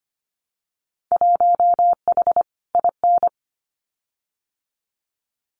Morse code, heard as a keyed tone, about 25 wpm, 700 Hz. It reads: 1H IN
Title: Winter Field Day Exchanges 25wpm